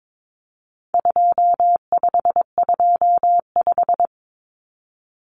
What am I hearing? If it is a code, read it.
Morse code, heard as a keyed tone, about 22 wpm, 700 Hz. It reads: 2525
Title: Common Cut Numbers - 4-Digit 22wpm